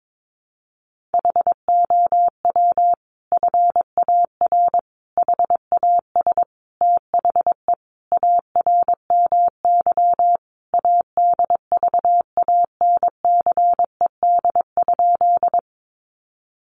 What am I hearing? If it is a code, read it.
Morse code, heard as a keyed tone, about 22 wpm, 700 Hz. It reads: HOW FAR HAS THE ARMY ADVANCED?
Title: Sentences from Top 1200 Words 22wpm